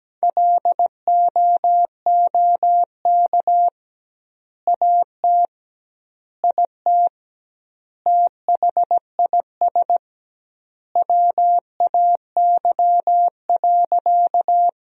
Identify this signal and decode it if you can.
Morse code, heard as a keyed tone, about 17 wpm, 700 Hz. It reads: LOOK AT IT THIS WAY.
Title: 2x Word-Spacing - Sentences from Top 100 Words - 17wpm